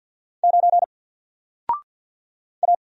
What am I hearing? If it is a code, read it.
Morse code, heard as a keyed tone, about 48 wpm, 700 Hz. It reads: 9 A
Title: Single Character (Letter:Number:Punctuation:Prosign) Alt 50wpm